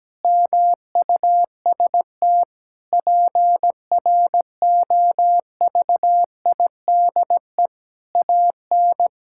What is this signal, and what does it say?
Morse code, heard as a keyed tone, about 17 wpm, 700 Hz. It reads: MUST PROVIDE AN
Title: Sentences from Top 1000 Words - Review Part 5 - 17wpm